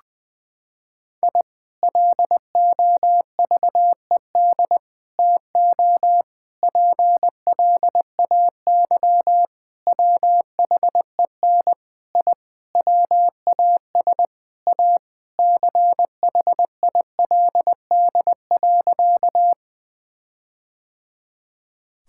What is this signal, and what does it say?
Morse code, heard as a keyed tone, about 20 wpm, 700 Hz. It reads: I LOVED TO PLAY WHEN I WAS A CHILD.
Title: Sentences with 4-Word Phrases - 20wpm